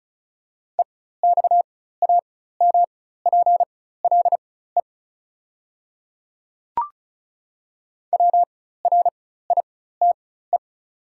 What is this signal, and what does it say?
Morse code, heard as a keyed tone, about 36 wpm, spaced as for 16 wpm, 690 Hz. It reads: EXAMPLE WRITE
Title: 4x Character Spacing - Top 300 Words - 35wpm